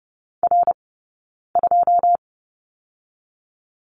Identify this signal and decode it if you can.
Morse code, heard as a keyed tone, about 30 wpm, 700 Hz. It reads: R 2
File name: 092 Lesson 16d - Letter c - Two Characters 30wpm